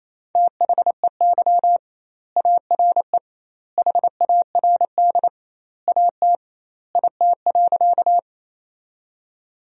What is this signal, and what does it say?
Morse code, heard as a keyed tone, about 28 wpm, 700 Hz. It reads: THEY ARE HARD AT IT.
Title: 2x Word-Spacing - Sentences from Top 300 Words - 28wpm